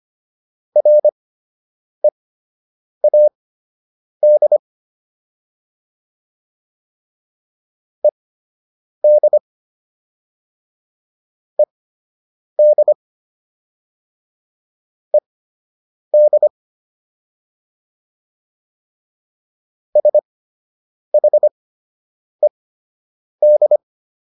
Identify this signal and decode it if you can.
Morse code, heard as a keyed tone, about 25 wpm, spaced as for 8 wpm, 600 Hz. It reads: READ ED ED ED SHED